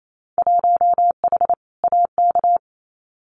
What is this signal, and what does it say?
Morse code, heard as a keyed tone, about 28 wpm, 700 Hz. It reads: 1H AK
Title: Winter Field Day Exchanges Encore 28wpm